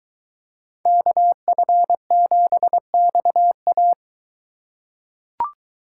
Morse code, heard as a keyed tone, about 23 wpm, 700 Hz. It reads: KF7XA